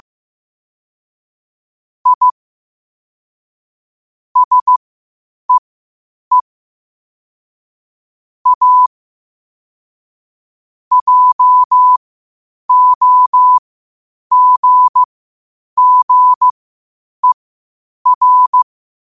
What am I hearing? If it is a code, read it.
Morse code, heard as a keyed tone, about 15 wpm, spaced as for 8 wpm, 1000 Hz. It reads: I SEE A JOGGER
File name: qotdfiles / 8wpm